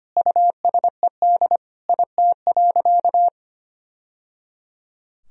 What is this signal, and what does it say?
Morse code, heard as a keyed tone, about 25 wpm, 700 Hz. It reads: USED IT.
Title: Sentences from Top 1000 Words - Review Part 2 - 25wpm